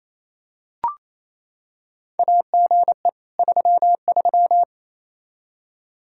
Morse code, heard as a keyed tone, about 28 wpm, 700 Hz. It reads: AGE 33
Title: Practice QSO - Exchange 3 - More Info - 28wpm